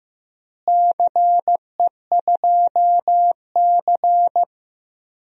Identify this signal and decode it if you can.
Morse code, heard as a keyed tone, about 16 wpm, 700 Hz. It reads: CE2C
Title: DXCC-100 Call Signs - 4 Characters - Repeated 2x - 15wpm